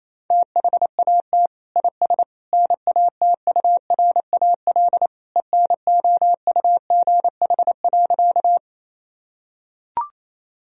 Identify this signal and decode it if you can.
Morse code, heard as a keyed tone, about 28 wpm, 700 Hz. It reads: THAT IS NATURAL ENOUGH.
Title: Sentences from Top 1000 Words - Review Part 4 - 28wpm